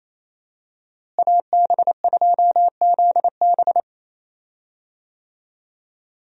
Morse code, heard as a keyed tone, about 27 wpm, 700 Hz. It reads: AB2ZB
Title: USA Call Signs - 5 Characters - Repeated 3x - 28wpm